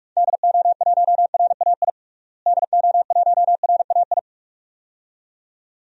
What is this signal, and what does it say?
Morse code, heard as a keyed tone, about 44 wpm, 700 Hz. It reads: DO1RAI DO1RAI
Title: DXCC-100 Call Signs - 6 Characters - Repeated 2x - 45wpm